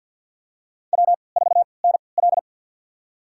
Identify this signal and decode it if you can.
Morse code, heard as a keyed tone, about 50 wpm, spaced as for 28 wpm, 700 Hz. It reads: W4NL
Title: USA Call Signs - 4 Characters - 3x Character Spacing - 50wpm